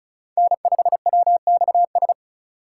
Morse code, heard as a keyed tone, about 36 wpm, 700 Hz. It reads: N5WXS
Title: USA Call Signs - 5 Characters - 3x Character Spacing - 35wpm